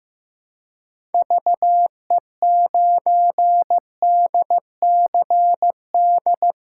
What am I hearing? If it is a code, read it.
Morse code, heard as a keyed tone, about 15 wpm, 700 Hz. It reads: VE9DCD
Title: K1USN SST QSO Practice - 15wpm